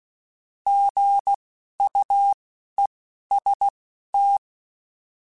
Morse code, heard as a keyed tone, about 16 wpm, spaced as for 11 wpm, 800 Hz. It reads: GUEST
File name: Alphabetical / Alphabetical mp3s / Gs @10